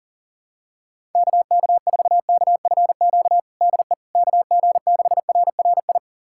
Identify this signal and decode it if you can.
Morse code, heard as a keyed tone, about 37 wpm, 700 Hz. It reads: KK4KFQ DE KG6RRI
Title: USA Callsign DE Callsign - 6 Characters - 40wpm